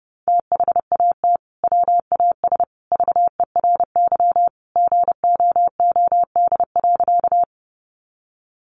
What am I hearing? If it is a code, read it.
Morse code, heard as a keyed tone, about 30 wpm, 700 Hz. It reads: THAT WAS VERY GOOD.
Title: Sentences from Top 200 Words 30wpm